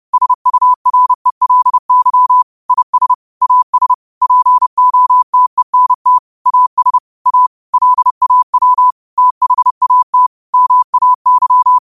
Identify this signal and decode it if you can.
Morse code, heard as a keyed tone, about 30 wpm, 1000 Hz. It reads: SURELY IS AS POTENT AS A LAW THAT MAY
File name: qotdfiles / 30wpm